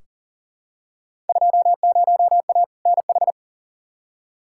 Morse code, heard as a keyed tone, about 40 wpm, 700 Hz. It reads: 20A NH
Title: Field Day Exchanges Encore - 40wpm